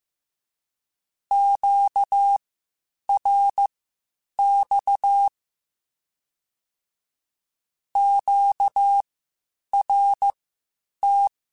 Morse code, heard as a keyed tone, about 15 wpm, spaced as for 8 wpm, 800 Hz. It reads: QRX QRT
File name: (spaced) / Q-Calls Spaced-mp3s / Q Calls Twice sp @13